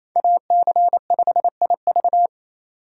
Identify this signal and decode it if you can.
Morse code, heard as a keyed tone, about 28 wpm, 700 Hz. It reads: AC5IV
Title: USA Call Signs - 3x Character Spacing - 28wpm